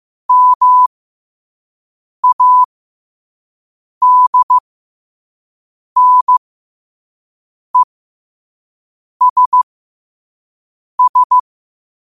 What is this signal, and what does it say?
Morse code, heard as a keyed tone, about 15 wpm, spaced as for 5 wpm, 1000 Hz. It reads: MADNESS
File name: qotdfiles / 5wpm